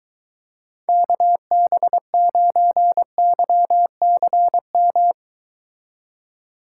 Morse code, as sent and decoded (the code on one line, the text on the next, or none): -.- -... ----. -.-- -.-. --
KB9YCM